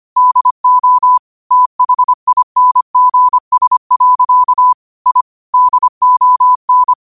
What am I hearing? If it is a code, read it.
Morse code, heard as a keyed tone, about 25 wpm, 1000 Hz. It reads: DO THINGS. I DON
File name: 25wpm